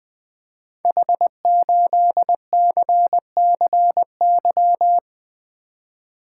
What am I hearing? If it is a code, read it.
Morse code, heard as a keyed tone, about 20 wpm, 700 Hz. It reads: H8CCY
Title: Code Groups - 5 Letter:Numbers - 20wpm